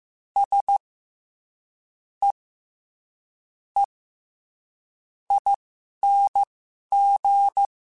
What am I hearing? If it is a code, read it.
Morse code, heard as a keyed tone, about 15 wpm, spaced as for 11 wpm, 800 Hz. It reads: S E E ING
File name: Special INGs-mp3s / Special INGs-mp3s / SlowToFaster ing @10